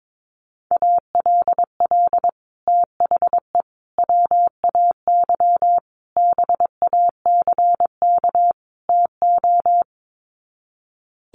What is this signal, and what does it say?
Morse code, heard as a keyed tone, about 22 wpm, 700 Hz. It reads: ALL THE WAY BACK TO
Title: Sets of 5 Words Encore 22wpm